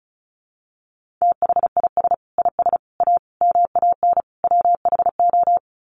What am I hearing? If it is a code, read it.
Morse code, heard as a keyed tone, about 34 wpm, 700 Hz. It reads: THIS IS A MAN WHO
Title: Sets of 5 Words Encore2 35wpm